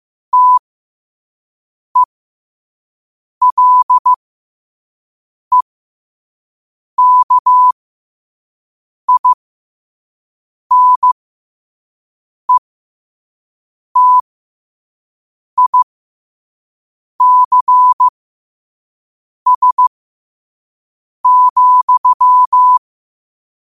Morse code, heard as a keyed tone, about 15 wpm, spaced as for 5 wpm, 1000 Hz. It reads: TELEKINETICS,
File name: qotdfiles / 5wpm